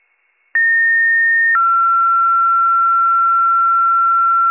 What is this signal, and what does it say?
Two-tone page, heard as a one-second tone, then a three-second tone, about 1850 then 1360 Hz. It is heard from 0.5 to 4.5 s.